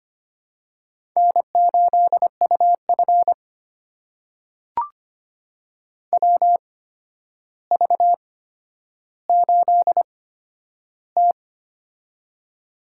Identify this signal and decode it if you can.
Morse code, heard as a keyed tone, about 25 wpm, spaced as for 11 wpm, 700 Hz. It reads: N8UF WV8T
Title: USA Call Signs - 4 Characters - 8x Character Spacing - 25wpm